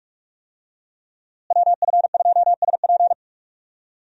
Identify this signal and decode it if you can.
Morse code, heard as a keyed tone, about 46 wpm, 700 Hz. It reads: WF2SP
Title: USA Call Signs - 5 Characters - Repeated 3x - 45wpm